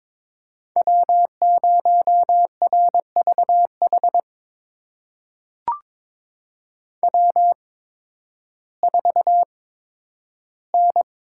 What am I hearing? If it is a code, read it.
Morse code, heard as a keyed tone, about 21 wpm, spaced as for 12 wpm, 700 Hz. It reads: W0RVH W4N
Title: USA Call Signs - 5 Characters - 8x Character Spacing - 22wpm